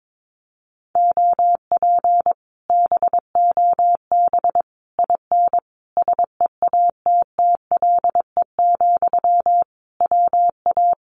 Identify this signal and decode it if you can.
Morse code, heard as a keyed tone, about 22 wpm, 700 Hz. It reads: OP BOB IN SEATTLE, WA
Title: Common QSO Phrases Encore 22wpm